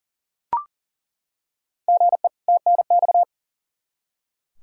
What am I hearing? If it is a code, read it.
Morse code, heard as a keyed tone, about 40 wpm, 700 Hz. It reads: GE TNX
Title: Practice QSO - Exchange 1 - Beginning - 40wpm